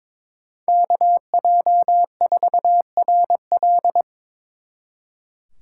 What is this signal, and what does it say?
Morse code, heard as a keyed tone, about 28 wpm, 700 Hz. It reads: KJ4RL